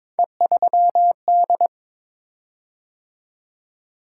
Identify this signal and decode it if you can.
Morse code, heard as a keyed tone, about 22 wpm, 700 Hz. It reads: E3D
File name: practice / 22wpm / Three Letter-Number Combinations - 22wpm